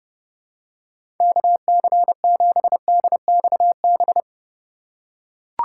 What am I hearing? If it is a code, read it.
Morse code, heard as a keyed tone, about 31 wpm, 700 Hz. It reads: KC7DXB
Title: USA Call Signs - 4x Character Spacing - 30wpm